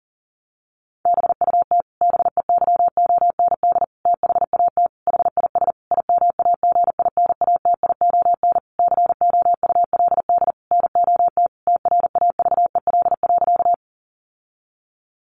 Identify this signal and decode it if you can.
Morse code, heard as a keyed tone, about 40 wpm, 700 Hz. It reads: BUT BEYOND THAT HIS IMAGINATION COULD NOT TRAVEL.
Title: Sentences from Top 1300 Words 40wpm